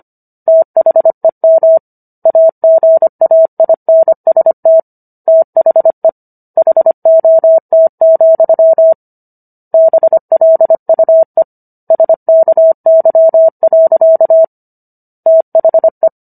Morse code, heard as a keyed tone, about 25 wpm, 650 Hz. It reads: THEM AGAINST THE HOT, BLUE SKY. THE